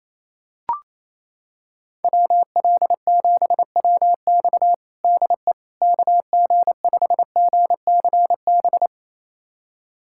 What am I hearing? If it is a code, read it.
Morse code, heard as a keyed tone, about 28 wpm, 700 Hz. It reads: WL7WX DE KG5GCB
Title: USA Callsign DE Callsign 28wpm